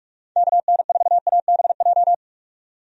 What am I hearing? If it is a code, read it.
Morse code, heard as a keyed tone, about 42 wpm, 700 Hz. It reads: KN4ABJ
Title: USA Call Signs - 6 Characters - 45wpm